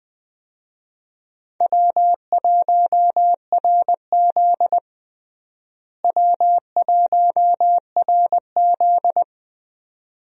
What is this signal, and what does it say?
Morse code, heard as a keyed tone, about 21 wpm, 700 Hz. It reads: W1RZ W1RZ
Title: USA Call Signs - 4 Characters - Repeated 2x - 20wpm